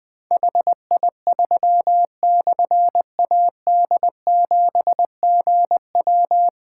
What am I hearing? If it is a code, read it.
Morse code, heard as a keyed tone, about 20 wpm, 700 Hz. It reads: HI3/AD7GW
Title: Difficult Call Signs 20wpm